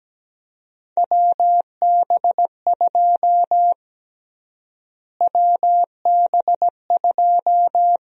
Morse code, heard as a keyed tone, about 17 wpm, 700 Hz. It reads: WB2 WB2
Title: USA Call Sign Prefixes - Repeated 2x - 17wpm